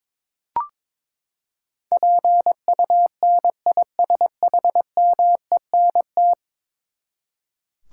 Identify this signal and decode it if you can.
Morse code, heard as a keyed tone, about 22 wpm, 700 Hz. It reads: PUNISHMENT
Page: morse-code-ninja-practice-22wpm